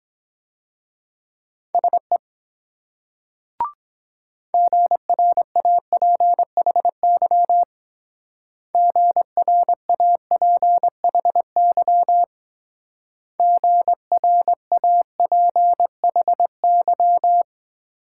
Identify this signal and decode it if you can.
Morse code, heard as a keyed tone, about 22 wpm, 700 Hz. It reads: SE GRAPHY GRAPHY GRAPHY